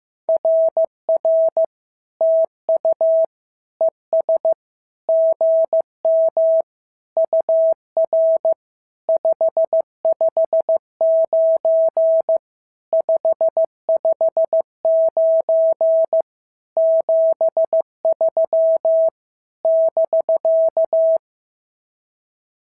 Morse code, heard as a keyed tone, about 15 wpm, 650 Hz. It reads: RR TU ES GM UR 559 559 73 <BK>
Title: SOTA QSO Practice 15wpm